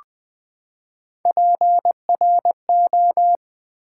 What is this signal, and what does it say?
Morse code, heard as a keyed tone, about 20 wpm, 700 Hz. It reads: PRO
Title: Build A Word - 3 Letters At A Time - 9-Letter Words - 20wpm